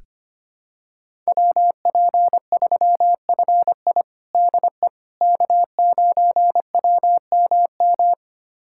Morse code, heard as a keyed tone, about 25 wpm, 700 Hz. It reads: WP3FI DE K9WMM
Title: USA Callsign DE Callsign - 5 Characters - 25wpm